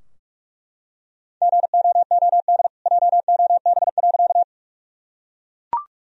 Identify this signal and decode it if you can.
Morse code, heard as a keyed tone, about 41 wpm, 700 Hz. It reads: GOOD JOB.
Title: Sentences from Top 1000 Words - Review Part 3 - 45wpm